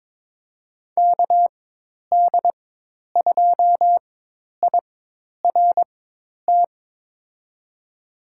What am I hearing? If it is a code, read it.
Morse code, heard as a keyed tone, about 22 wpm, spaced as for 10 wpm, 700 Hz. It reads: KD2IRT
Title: USA Call Signs - 6 Characters - 4x Character Spacing - 22wpm